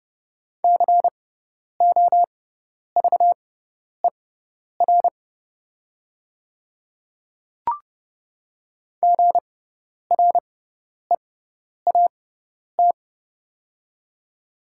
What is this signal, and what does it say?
Morse code, heard as a keyed tone, about 30 wpm, spaced as for 10 wpm, 700 Hz. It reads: COVER GREAT
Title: 6x Character Spacing - Top 300 Words - 30wpm